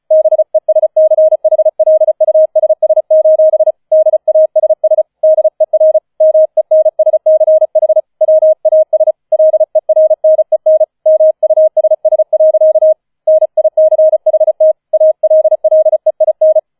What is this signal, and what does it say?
Morse code, heard as a keyed tone, about 35 wpm, 630 Hz. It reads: BESCHLUSS: DASS DER MENSCH WAS LERNEN MUSS. NICHT ALLEIN
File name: 004 Vierter Streich 35wpm